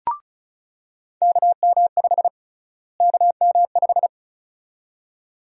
Morse code, heard as a keyed tone, about 36 wpm, 700 Hz. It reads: KM5 KM5
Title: USA Call Sign Prefixes - Repeated 2x - 35wpm